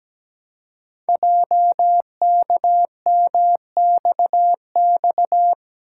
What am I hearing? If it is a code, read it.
Morse code, heard as a keyed tone, about 17 wpm, 700 Hz. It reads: JKMXX
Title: Code Groups - 5 Letters - 17wpm